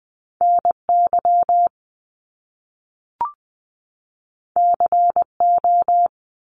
Morse code, heard as a keyed tone, about 20 wpm, 700 Hz. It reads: NY CO
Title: US States 20wpm